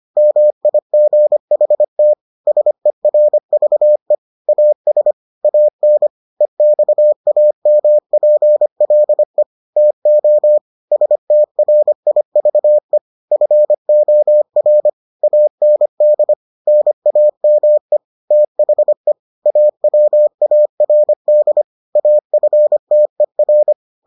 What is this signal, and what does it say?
Morse code, heard as a keyed tone, about 25 wpm, 600 Hz. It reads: MIGHT SERVE AS AN EXAMPLE TO STRIVE FOR AND NAME THE AWARD AFTER